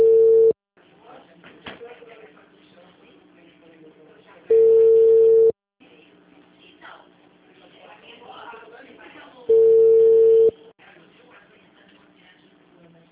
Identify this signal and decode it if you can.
Ringback tone (China, 450 Hz), with speech.
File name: out- -2022- - - .85122